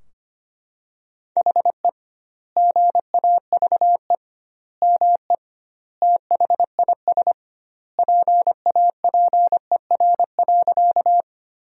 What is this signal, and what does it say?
Morse code, heard as a keyed tone, about 25 wpm, 700 Hz. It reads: HE GAVE ME THIS PAPER.
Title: 2x Word-Spacing - Sentences from Top 500 Words - 25wpm